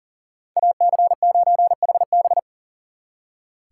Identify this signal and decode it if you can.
Morse code, heard as a keyed tone, about 52 wpm, 700 Hz. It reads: AC9HB